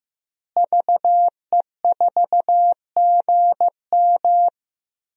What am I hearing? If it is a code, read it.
Morse code, heard as a keyed tone, about 15 wpm, 700 Hz. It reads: VE4GM
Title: DXCC-100 Call Signs - 5 Characters - Repeated 3x - 15wpm